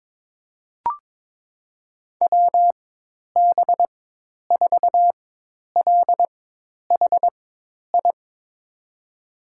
Morse code, heard as a keyed tone, about 22 wpm, spaced as for 10 wpm, 700 Hz. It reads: WB4LHI
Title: USA Call Signs - 6 Characters - 4x Character Spacing - 22wpm